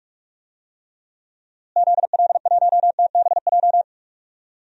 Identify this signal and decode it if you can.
Morse code, heard as a keyed tone, about 45 wpm, 700 Hz. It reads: ZL1TBJ
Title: DXCC-100 Call Signs - 6 Characters - Repeated 2x - 45wpm